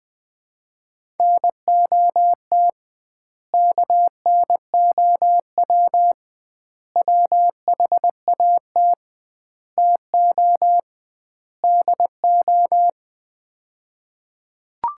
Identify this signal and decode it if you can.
Morse code, heard as a keyed tone, about 20 wpm, 700 Hz. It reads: NOT KNOW WHAT TO DO
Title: 2x Word-Spacing - Sets of 5 Words - 20wpm